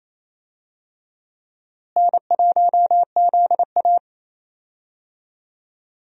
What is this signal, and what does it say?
Morse code, heard as a keyed tone, about 28 wpm, 700 Hz. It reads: N1ZA
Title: USA Call Signs - 4 Characters - Repeated 2x - 28wpm